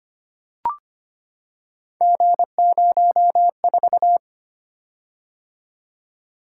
Morse code, heard as a keyed tone, about 25 wpm, 700 Hz. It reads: G04
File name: Three Letter-Number Combinations - 25wpm